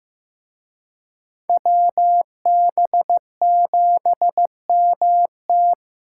Morse code, heard as a keyed tone, about 18 wpm, 700 Hz. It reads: WB7MT
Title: Speed Racing - USA 2x2 Call Signs 15wpm